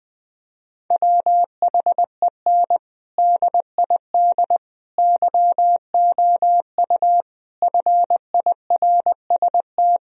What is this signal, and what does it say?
Morse code, heard as a keyed tone, about 21 wpm, 700 Hz. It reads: WHEN DID YOU FIRST
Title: Sentences from Top 1000 Words - Review Part 2 - 20wpm